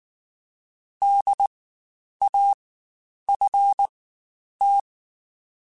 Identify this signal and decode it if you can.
Morse code, heard as a keyed tone, about 19 wpm, spaced as for 9 wpm, 800 Hz. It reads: DAFT
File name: Spaced Alphas / Spaced Alphas mp3s / Ds sp @18